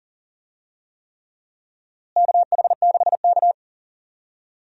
Morse code, heard as a keyed tone, about 40 wpm, 700 Hz. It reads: KH6K
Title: DXCC-100 Call Signs - 4 Characters - Repeated 3x - 40wpm